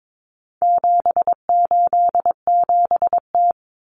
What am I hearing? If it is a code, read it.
Morse code, heard as a keyed tone, about 22 wpm, 700 Hz. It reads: 787T
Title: Common Cut Numbers - 4-Digit 22wpm